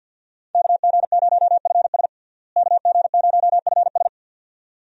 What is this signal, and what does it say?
Morse code, heard as a keyed tone, about 52 wpm, 700 Hz. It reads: KG0FS KG0FS
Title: USA Call Signs - 5 Characters - Repeated 2x - 50wpm